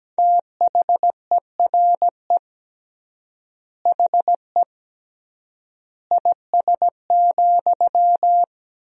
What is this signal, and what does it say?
Morse code, heard as a keyed tone, about 17 wpm, 700 Hz. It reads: THERE HE IS,
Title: 3x Word-Spacing - Sentences from Top 100 Words - 17wpm